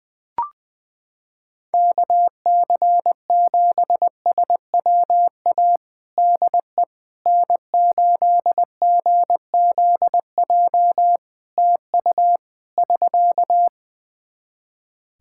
Morse code, heard as a keyed tone, about 20 wpm, 700 Hz. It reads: KC7SWA DE N8GZJ TU <SK>
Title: Practice QSO - Exchange 4 - Ending - 20wpm